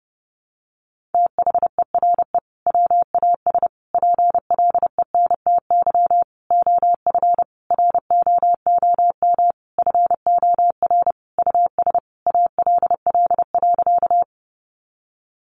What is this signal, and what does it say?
Morse code, heard as a keyed tone, about 30 wpm, 700 Hz. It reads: THERE WAS PLENTY OF ROOM FOR US ALL.
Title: Sentences from Top 1300 Words 30wpm